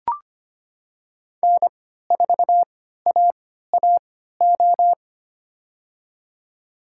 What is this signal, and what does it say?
Morse code, heard as a keyed tone, about 25 wpm, spaced as for 14 wpm, 700 Hz. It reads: N4AAO
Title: USA Call Signs - 5 Characters - 3x Character Spacing - 25wpm